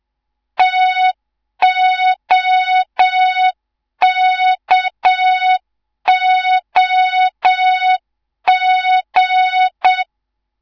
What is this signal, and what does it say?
Morse code, heard as a keyed tone, about 7 wpm, 3700 Hz, with noise. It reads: TOKOG
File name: Morze-HA2MN / hang / 18 IRASJELEK2